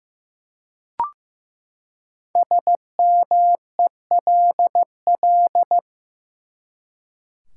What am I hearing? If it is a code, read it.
Morse code, heard as a keyed tone, about 15 wpm, 700 Hz. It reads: SMELL
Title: Five Letter Words - 15wpm